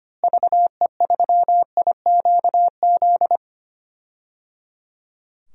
Morse code, heard as a keyed tone, about 25 wpm, 700 Hz. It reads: VE3IQZ
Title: DXCC-100 Call Signs - 6 Characters - Repeated 3x - 25wpm